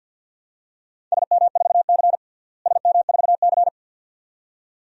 Morse code, heard as a keyed tone, about 50 wpm, 700 Hz. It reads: SM4C SM4C
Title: DXCC-100 Call Signs - 4 Characters - Repeated 2x - 50wpm